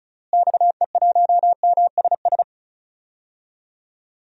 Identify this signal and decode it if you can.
Morse code, heard as a keyed tone, about 36 wpm, 700 Hz. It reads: XE1MSS
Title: DXCC-100 Call Signs - 6 Characters - Repeated 3x - 35wpm